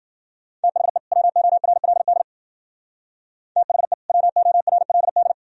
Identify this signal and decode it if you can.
Morse code, heard as a keyed tone, about 49 wpm, 700 Hz. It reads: THE WORLD THE WORLD
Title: Speed-Racing - Sets of 2 Words 40wpm